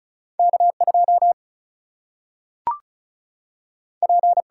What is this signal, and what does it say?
Morse code, heard as a keyed tone, about 36 wpm, 700 Hz. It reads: K2 P
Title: Single-Letter Single-Number 35wpm